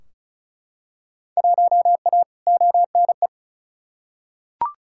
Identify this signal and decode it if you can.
Morse code, heard as a keyed tone, about 34 wpm, 700 Hz. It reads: 1A ONE
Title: Field Day Exchanges - 35wpm